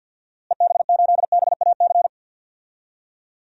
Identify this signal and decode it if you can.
Morse code, heard as a keyed tone, about 53 wpm, 700 Hz. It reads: EB8BAC
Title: DXCC-100 Call Signs - 6 Characters - Repeated 2x - 50wpm